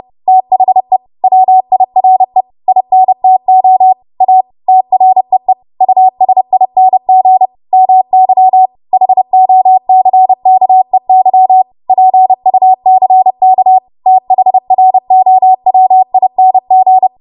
Morse code, heard as a keyed tone, about 30 wpm, 750 Hz. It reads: THE WIRE INTO A TREE USING MY HOCKEY PUCK THROWING
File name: Morse / 30WPM